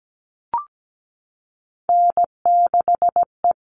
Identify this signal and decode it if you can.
Morse code, heard as a keyed tone, about 17 wpm, 700 Hz. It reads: N6E
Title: All Cut Numbers - 3-Digit 17wpm